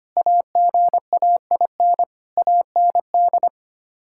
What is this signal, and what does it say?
Morse code, heard as a keyed tone, about 25 wpm, 700 Hz. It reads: AGAIN AND
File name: Sets of 9 Words - 25wpm